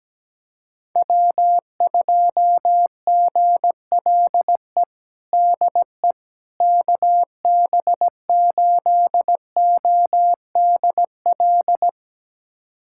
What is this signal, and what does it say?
Morse code, heard as a keyed tone, about 17 wpm, 700 Hz. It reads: W2GLE DE KB8ODL
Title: USA Callsign DE Callsign 17wpm